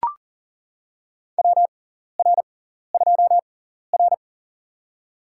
Morse code, heard as a keyed tone, about 40 wpm, spaced as for 14 wpm, 700 Hz. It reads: WR2R
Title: USA Call Signs - 4 Characters - 6x Character Spacing - 40wpm